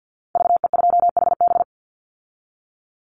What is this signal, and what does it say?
Morse code, heard as a keyed tone, about 50 wpm, 700 Hz. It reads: VE2HB
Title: Call Signs - 50wpm